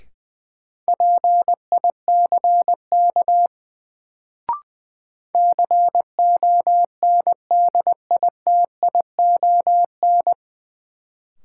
Morse code, heard as a keyed tone, about 21 wpm, 700 Hz. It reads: PICK CONDITION
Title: Sentences from Top 1000 Words - New Word Drill - 20wpm